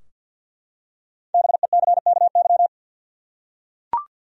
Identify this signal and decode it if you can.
Morse code, heard as a keyed tone, about 52 wpm, 700 Hz. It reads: BECKY
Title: USA Common Names - Max 5 Letters - 50wpm